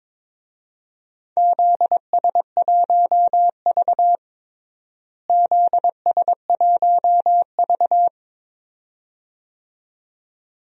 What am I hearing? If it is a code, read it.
Morse code, heard as a keyed tone, about 22 wpm, 700 Hz. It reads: ZS1V ZS1V
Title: DXCC-100 Call Signs - 4 Characters - Repeated 2x - 22wpm